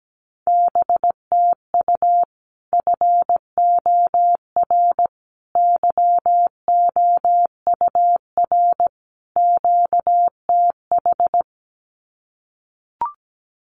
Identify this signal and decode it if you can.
Morse code, heard as a keyed tone, about 17 wpm, 700 Hz. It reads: BTU FOR YOUR QTH
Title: Common QSO Phrases 17wpm